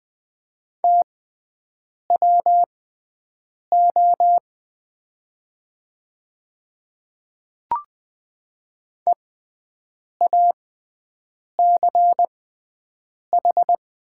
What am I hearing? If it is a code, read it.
Morse code, heard as a keyed tone, about 20 wpm, spaced as for 7 wpm, 700 Hz. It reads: TWO EACH
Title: 6x Character Spacing - Top 200 Words - 20wpm